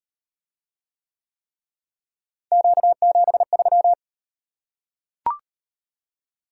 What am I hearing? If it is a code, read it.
Morse code, heard as a keyed tone, about 38 wpm, 700 Hz. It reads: Q73